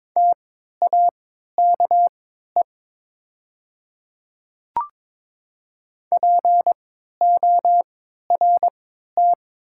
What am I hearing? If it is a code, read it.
Morse code, heard as a keyed tone, about 22 wpm, spaced as for 12 wpm, 700 Hz. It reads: TAKE PORT
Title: 3x Character Spacing - Top 200 Words - 22wpm